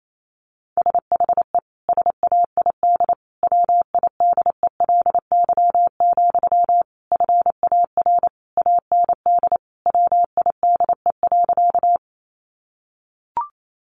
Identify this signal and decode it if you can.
Morse code, heard as a keyed tone, about 28 wpm, 700 Hz. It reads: SHE SAID WIDELY, FAR AND WIDE.
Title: Sentences from Top 1200 Words 28wpm